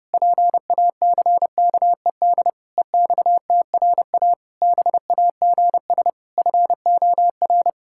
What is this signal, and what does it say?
Morse code, heard as a keyed tone, about 32 wpm, 700 Hz. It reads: PACKED EXTRA BAGS FOR
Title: Pangram Challenge v2 - 30wpm